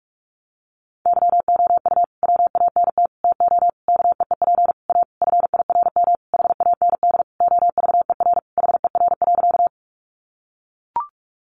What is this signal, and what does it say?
Morse code, heard as a keyed tone, about 44 wpm, 700 Hz. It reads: YOU WANT TO KEEP A FIRM HAND OVER HER.
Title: Sentences from Top 1400 Words 45wpm